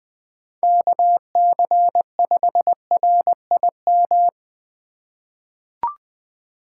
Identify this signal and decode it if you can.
Morse code, heard as a keyed tone, about 21 wpm, 700 Hz. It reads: KC5RIM
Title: USA Call Signs - 6x Character Spacing - 20wpm